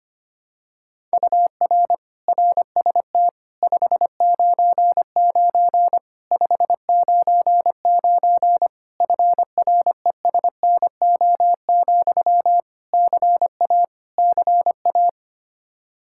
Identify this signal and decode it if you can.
Morse code, heard as a keyed tone, about 25 wpm, 700 Hz. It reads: UR RST 599 599 FRESNO, CA CA
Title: Practice QSO - Exchange 1 - Beginning - 25wpm